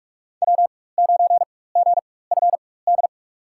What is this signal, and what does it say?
Morse code, heard as a keyed tone, about 45 wpm, spaced as for 21 wpm, 700 Hz. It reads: W9GFD
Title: USA Call Signs - 4x Character Spacing - 45wpm